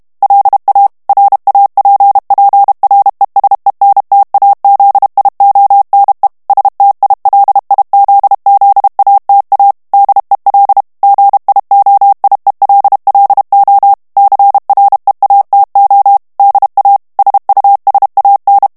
Morse code, heard as a keyed tone, about 32 wpm, 800 Hz. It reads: LA RAPPRESENTAZIONE STILIZZATA DEL GIOIELLO CREATO DA SUSAN
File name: ITALIEN texte clair 1 vitesse 30 mots minute